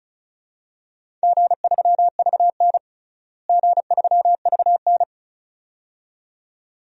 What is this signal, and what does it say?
Morse code, heard as a keyed tone, about 36 wpm, 700 Hz. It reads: G3VN G3VN
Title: DXCC-100 Call Signs - 4 Characters - Repeated 2x - 35wpm